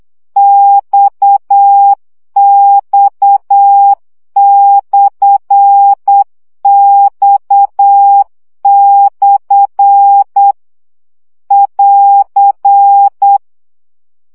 Morse code, heard as a keyed tone, about 8 wpm, 800 Hz. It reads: XX/X/ <AR>
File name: CW gyakorlas x per 7 baud